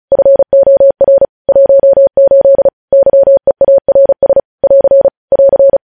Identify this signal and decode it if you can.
Morse code, heard as a keyed tone, about 36 wpm, 550 Hz. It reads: FOR 18 YEARS <AR> <AR>
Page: News headlines in Morse code at 35 wpm